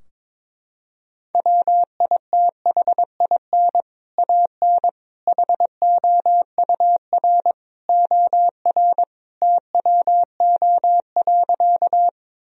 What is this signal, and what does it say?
Morse code, heard as a keyed tone, about 22 wpm, 700 Hz. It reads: WITHIN AN HOUR OR TWO.
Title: Sentences from Top 1000 Words - Review Part 4 - 22wpm